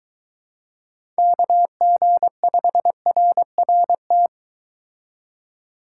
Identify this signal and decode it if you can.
Morse code, heard as a keyed tone, about 23 wpm, 700 Hz. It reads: KG5RRT